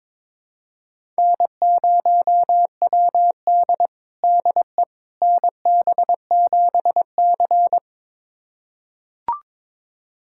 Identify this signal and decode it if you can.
Morse code, heard as a keyed tone, about 23 wpm, 700 Hz. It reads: N0WD DE NB7C
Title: USA Callsign DE Callsign - 4 Characters - 22wpm